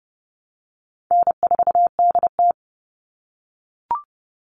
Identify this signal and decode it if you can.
Morse code, heard as a keyed tone, about 30 wpm, 700 Hz. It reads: N4DT
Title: Easy Call Signs - 30wpm